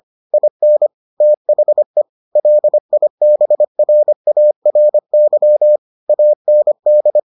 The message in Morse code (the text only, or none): IN THE LIBRARY AND